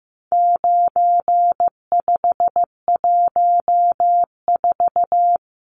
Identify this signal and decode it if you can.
Morse code, heard as a keyed tone, about 15 wpm, 700 Hz. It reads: 9514
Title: Numbers - 4-Digit 15wpm